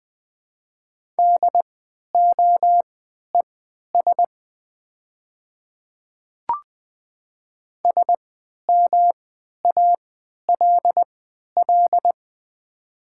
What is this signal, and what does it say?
Morse code, heard as a keyed tone, about 20 wpm, spaced as for 11 wpm, 700 Hz. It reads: DOES SMALL
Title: 3x Character Spacing - Top 300 Words - 20wpm